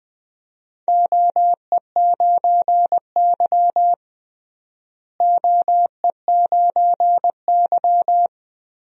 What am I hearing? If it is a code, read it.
Morse code, heard as a keyed tone, about 20 wpm, 700 Hz. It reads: OE9Y OE9Y
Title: DXCC-100 Call Signs - 4 Characters - Repeated 2x - 20wpm